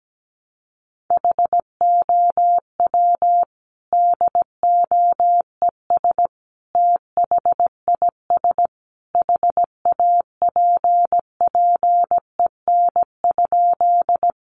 Morse code, heard as a keyed tone, about 17 wpm, 700 Hz. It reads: HOW DOES THIS HAPPEN?
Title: Sentences from Top 400 Words 17wpm